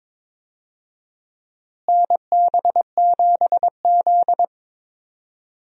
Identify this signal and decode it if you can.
Morse code, heard as a keyed tone, about 21 wpm, 700 Hz. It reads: NB7Z
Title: USA Call Signs - 4 Characters - Repeated 2x - 22wpm